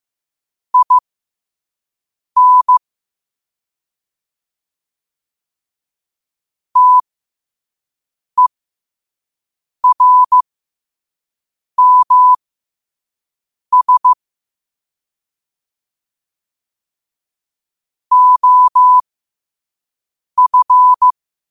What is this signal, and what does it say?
Morse code, heard as a keyed tone, about 15 wpm, spaced as for 5 wpm, 1000 Hz. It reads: IN TERMS OF
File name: qotdfiles / 5wpm